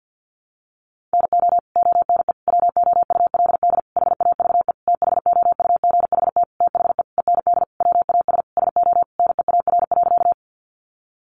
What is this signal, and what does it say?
Morse code, heard as a keyed tone, about 49 wpm, 700 Hz. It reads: NO ONE WOULD HAVE THOUGHT THE END WAS SO NEAR.
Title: Sentences from Top 300 Words 50wpm